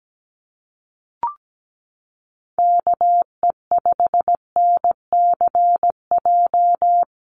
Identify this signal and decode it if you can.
Morse code, heard as a keyed tone, about 17 wpm, 700 Hz. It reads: KE5NCJ
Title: USA Call Signs Repeated 3x 17wpm